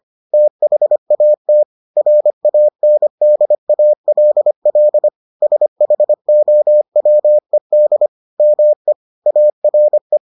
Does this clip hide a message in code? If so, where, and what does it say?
Yes: Morse code from 0.3 to 10.2 s, heard as a keyed tone, about 25 wpm, 600 Hz. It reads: THAT RANDALL SHOWED ME ARE